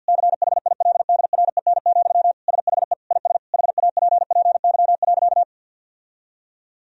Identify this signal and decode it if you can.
Morse code, heard as a keyed tone, about 50 wpm, 700 Hz. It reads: CHILDREN, SHE IS HAPPY.
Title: Sentences from Top 1000 Words - Review Part 3 - 50wpm